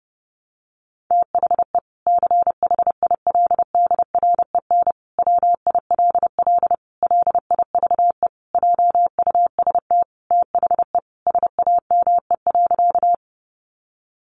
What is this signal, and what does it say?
Morse code, heard as a keyed tone, about 30 wpm, 700 Hz. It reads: THE CHILDREN WILL LIVE JUST THE SAME.
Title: Sentences from Top 300 Words 30wpm